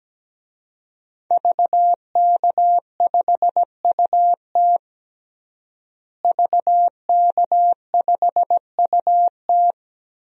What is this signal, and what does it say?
Morse code, heard as a keyed tone, about 17 wpm, 700 Hz. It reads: VK5UT VK5UT
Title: DXCC-100 Call Signs - 5 Characters - Repeated 2x - 17wpm